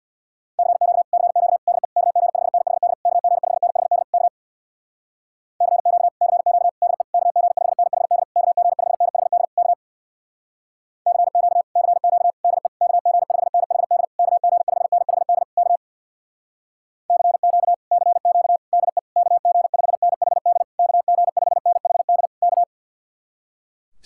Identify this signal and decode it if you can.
Morse code, heard as a keyed tone, about 60 wpm, 700 Hz. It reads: CQ CQ DE KG5NHD KG5NHD K CQ CQ DE KG5NHD KG5NHD K CQ CQ DE KG5NHD KG5NHD K CQ CQ DE KG5NHD KG5NHD K
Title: Speed-Racing Common QSO Phrases - 50wpm